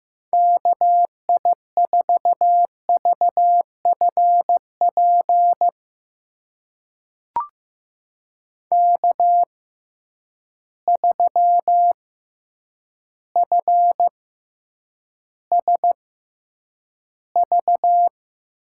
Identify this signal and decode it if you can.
Morse code, heard as a keyed tone, about 15 wpm, spaced as for 8 wpm, 700 Hz. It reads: KI4VFP K3FSV
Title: USA Call Signs - 6x Character Spacing - 15wpm